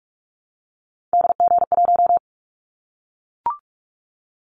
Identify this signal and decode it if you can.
Morse code, heard as a keyed tone, about 44 wpm, 700 Hz. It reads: DG1
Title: Callsign Prefixes for Top 25 DXCC - Repeated 3x - 45wpm